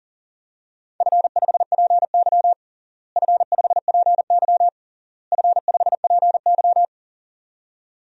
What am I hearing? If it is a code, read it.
Morse code, heard as a keyed tone, about 40 wpm, 700 Hz. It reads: F5PY F5PY F5PY
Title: DXCC-100 Call Signs - 4 Characters - Repeated 3x - 40wpm